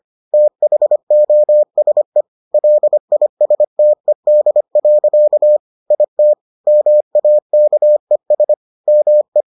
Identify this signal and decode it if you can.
Morse code, heard as a keyed tone, about 25 wpm, 600 Hz. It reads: THOSE LISTED. IT MAKES ME